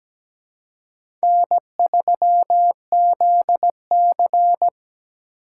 Morse code, heard as a keyed tone, about 17 wpm, 700 Hz. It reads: N3ZC
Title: USA Call Signs - 4 Characters - 2x Character Spacing - 17wpm